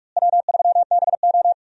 Morse code, heard as a keyed tone, about 43 wpm, 700 Hz. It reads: W3BO
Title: USA Call Signs - 4 Characters - Repeated 2x - 45wpm